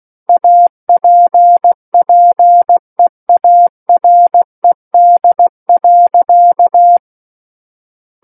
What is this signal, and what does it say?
Morse code, heard as a keyed tone, about 16 wpm, 700 Hz. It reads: APPEARED.
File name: Book One / 07-Chapter 7 - 16 WPM